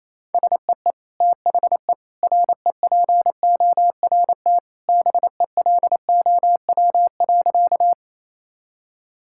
Morse code, heard as a keyed tone, about 28 wpm, 700 Hz. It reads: SEE THE REPORT BELOW.
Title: Sentences from Top 1000 Words - Review Part 6 - 28wpm